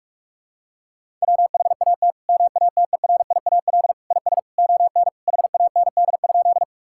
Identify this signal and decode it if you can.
Morse code, heard as a keyed tone, about 41 wpm, 700 Hz. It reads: WHAT MATERIAL IS ON HAND?
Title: Sentences from Top 1000 Words - Review Part 5 - 45wpm